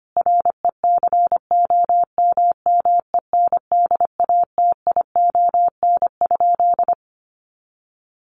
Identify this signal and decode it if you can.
Morse code, heard as a keyed tone, about 25 wpm, 700 Hz. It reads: RECOMMENDATION?
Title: Sentences from Top 1400 Words 25wpm